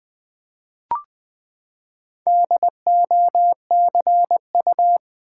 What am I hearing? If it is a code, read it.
Morse code, heard as a keyed tone, about 20 wpm, 700 Hz. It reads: DOCU
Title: Build A Word - 4 Letters At A Time - 11-Letter Words - 20wpm